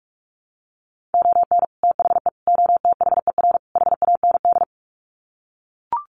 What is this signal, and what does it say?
Morse code, heard as a keyed tone, about 44 wpm, 700 Hz. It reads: ON THE OTHER HAND
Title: Sets of 4 Words 45wpm